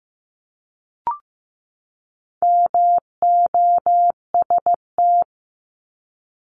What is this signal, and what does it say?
Morse code, heard as a keyed tone, about 15 wpm, 700 Hz. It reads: MOST
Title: Top 200 Words 15wpm